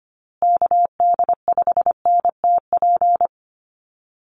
Morse code, heard as a keyed tone, about 25 wpm, 700 Hz. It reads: KD5NTP
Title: USA Call Signs Repeated 2x 25wpm